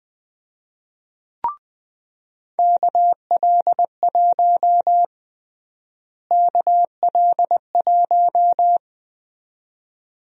Morse code, heard as a keyed tone, about 21 wpm, 700 Hz. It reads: KL1 KL1
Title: USA Call Sign Prefixes - Repeated 2x - 20wpm